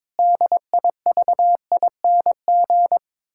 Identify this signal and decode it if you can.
Morse code, heard as a keyed tone, about 22 wpm, 700 Hz. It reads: DIVING
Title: Pangram Challenge v2 - 22wpm